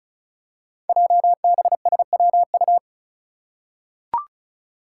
Morse code, heard as a keyed tone, about 36 wpm, 700 Hz. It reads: JBSWU
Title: Code Groups - 5 Letter:Numbers - 35wpm